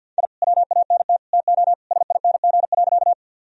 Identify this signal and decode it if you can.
Morse code, heard as a keyed tone, about 50 wpm, 700 Hz. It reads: I WANT TO SING.
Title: Sentences from Top 1000 Words - Review Part 2 - 50wpm